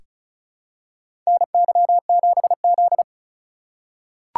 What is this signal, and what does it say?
Morse code, heard as a keyed tone, about 34 wpm, 700 Hz. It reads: NY7Z
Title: K1USN SST QSO Practice - 35wpm